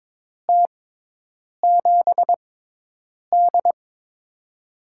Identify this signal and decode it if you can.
Morse code, heard as a keyed tone, about 22 wpm, spaced as for 8 wpm, 700 Hz. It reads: T7D
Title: USA Call Signs - 6x Character Spacing - 22wpm